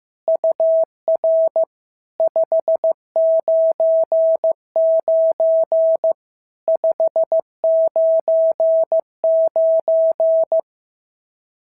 Morse code, heard as a keyed tone, about 14 wpm, 650 Hz. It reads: UR 599 599
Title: SOTA QSO Practice 15wpm